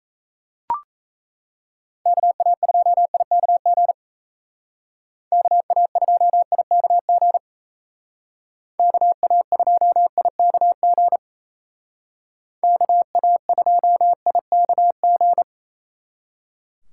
Morse code, heard as a keyed tone, about 37 wpm, 700 Hz. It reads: KA2IKG KA2IKG KA2IKG KA2IKG